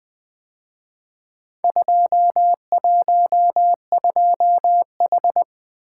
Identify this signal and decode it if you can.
Morse code, heard as a keyed tone, about 23 wpm, 700 Hz. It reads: 212H
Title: Speed-Racing - Four Letter-Number Combinations - 20wpm